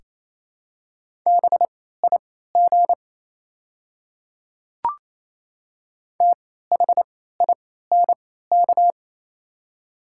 Morse code, heard as a keyed tone, about 28 wpm, spaced as for 16 wpm, 700 Hz. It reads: BIG THINK
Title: 3x Character Spacing - Top 200 Words - 28wpm